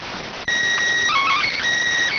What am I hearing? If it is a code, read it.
SSTV (slow-scan television), PD 290.